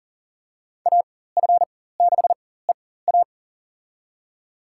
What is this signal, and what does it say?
Morse code, heard as a keyed tone, about 40 wpm, spaced as for 19 wpm, 700 Hz. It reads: AF6EA
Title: USA Call Signs - 5 Characters - 4x Character Spacing - 40wpm